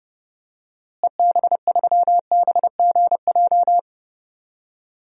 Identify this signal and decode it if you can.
Morse code, heard as a keyed tone, about 30 wpm, 700 Hz. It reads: EB3BGJ
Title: DXCC-100 Call Signs - 6 Characters - Repeated 3x - 30wpm